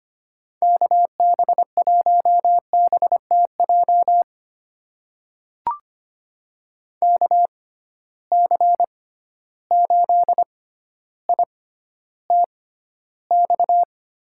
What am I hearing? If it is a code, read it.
Morse code, heard as a keyed tone, about 25 wpm, spaced as for 13 wpm, 700 Hz. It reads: KB1BTJ KC8ITX
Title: USA Call Signs - 6 Characters - 6x Character Spacing - 25wpm